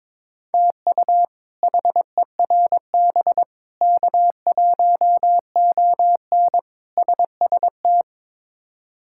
Morse code, heard as a keyed tone, about 23 wpm, 700 Hz. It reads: TU HERB K1ON SST
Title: K1USN SST QSO Practice - 22wpm